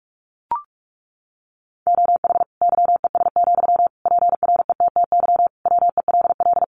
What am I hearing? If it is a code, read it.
Morse code, heard as a keyed tone, about 44 wpm, 700 Hz. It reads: OH YES, PRETTY WELL
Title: Sentences from Top 1000 Words 45wpm